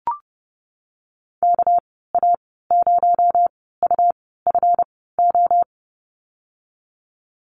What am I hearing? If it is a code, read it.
Morse code, heard as a keyed tone, about 30 wpm, spaced as for 17 wpm, 700 Hz. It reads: KA0UFO
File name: Lesson 37g - Letter x - Call Signs 30wpm